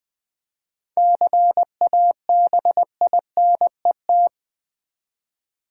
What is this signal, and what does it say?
Morse code, heard as a keyed tone, about 20 wpm, 700 Hz. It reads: CABINET
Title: Seven Letter Words - 20wpm